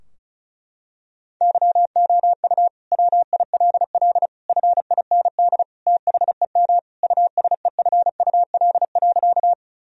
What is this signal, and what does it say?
Morse code, heard as a keyed tone, about 36 wpm, 700 Hz. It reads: YOU WILL FIND THEM USEFUL.
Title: Sentences from Top 1000 Words - Review Part 4 - 35wpm